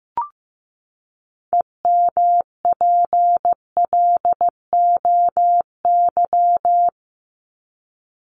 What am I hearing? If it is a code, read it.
Morse code, heard as a keyed tone, about 15 wpm, 700 Hz. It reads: EMPLOY
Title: ED Suffix - Top 100 Words with Base Words 15wpm